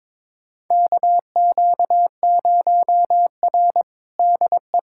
Morse code, heard as a keyed tone, about 22 wpm, 700 Hz. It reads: KQ0R DE
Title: USA Callsign DE Callsign 22wpm